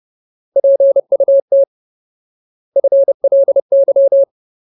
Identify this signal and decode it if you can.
Morse code, heard as a keyed tone, about 30 wpm, 550 Hz. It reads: PUT FLY
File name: Ultimate Word List / 30WPM Ultimate